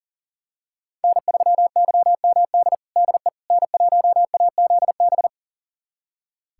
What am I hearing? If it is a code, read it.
Morse code, heard as a keyed tone, about 40 wpm, 700 Hz. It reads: N3YMD DE N1AZB
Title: USA Callsign DE Callsign - 5 Characters - 40wpm